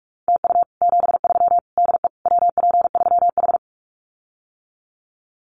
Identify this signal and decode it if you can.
Morse code, heard as a keyed tone, about 44 wpm, 700 Hz. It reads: TU 73 DE WP3H
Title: Common QSO Phrases Encore 45wpm